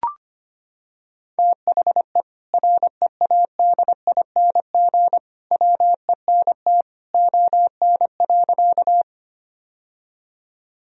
Morse code, heard as a keyed tone, about 25 wpm, 700 Hz. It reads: THE READING WENT ON.
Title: Sentences from Top 1000 Words - Review Part 3 - 25wpm